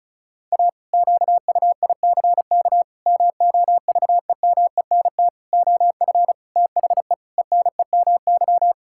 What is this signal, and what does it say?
Morse code, heard as a keyed tone, about 36 wpm, 700 Hz. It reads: A QUICK MOVEMENT OF THE ENEMY
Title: Pangram Challenge v2 - 35wpm